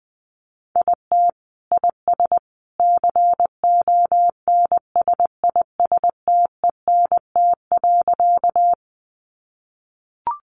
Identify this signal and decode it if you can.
Morse code, heard as a keyed tone, about 20 wpm, 700 Hz. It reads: IT IS CONSISTENT.
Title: Sentences from Top 1500 Words 20wpm